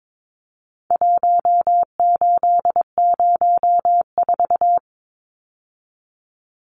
Morse code, heard as a keyed tone, about 22 wpm, 700 Hz. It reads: 1804
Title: Numbers - Contest Style - 4-Digit 22wpm